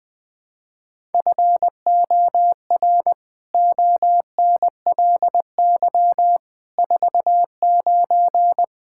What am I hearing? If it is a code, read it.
Morse code, heard as a keyed tone, about 21 wpm, 700 Hz. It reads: FOR ONLY 49
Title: Pangram Challenge v2 - 20wpm